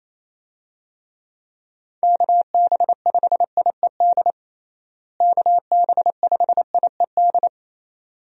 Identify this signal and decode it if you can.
Morse code, heard as a keyed tone, about 28 wpm, 700 Hz. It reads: KB5IED KB5IED
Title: USA Call Signs - 6 Characters - Repeated 2x - 28wpm